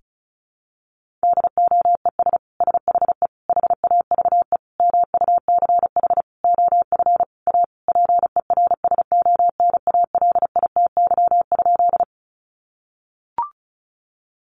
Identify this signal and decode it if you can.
Morse code, heard as a keyed tone, about 34 wpm, 700 Hz. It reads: DOES SHE HAVE MUCH OF A PERSONALITY?
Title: Sentences from Top 1400 Words 35wpm